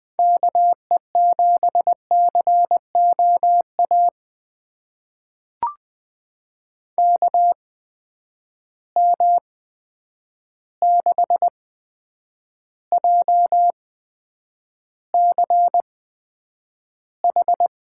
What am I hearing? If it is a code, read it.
Morse code, heard as a keyed tone, about 20 wpm, spaced as for 9 wpm, 700 Hz. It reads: KE7COA KM6JCH
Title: USA Call Signs - 6 Characters - 8x Character Spacing - 20wpm